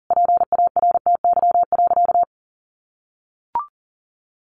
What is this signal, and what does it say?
Morse code, heard as a keyed tone, about 40 wpm, 700 Hz. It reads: PARTY.
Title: Sentences from Top 1400 Words 40wpm